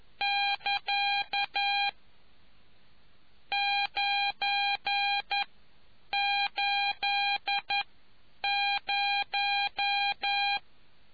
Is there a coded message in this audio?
Yes: Morse code from 0.2 to 10.6 s, heard as a keyed tone, about 11 wpm, spaced as for 8 wpm, 800 Hz. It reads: <KA> 980